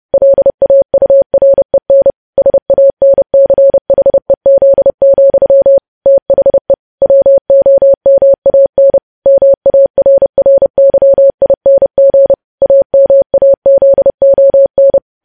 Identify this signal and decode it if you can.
Morse code, heard as a keyed tone, about 30 wpm, 550 Hz. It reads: LAUREN SANCHEZ, THE WOMAN MARRYING AMAZON
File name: podcast / News headlines-1x-30x30